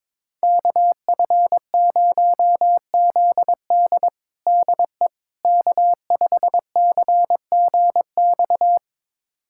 Morse code, heard as a keyed tone, about 22 wpm, 700 Hz. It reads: KF0ZD DE K5CGX
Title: USA Callsign DE Callsign - 5 Characters - 22wpm